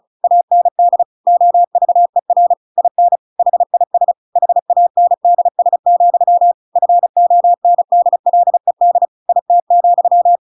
Morse code, heard as a keyed tone, about 34 wpm, 700 Hz. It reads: AND OVER IN HIS HANDS, FONDLED IT,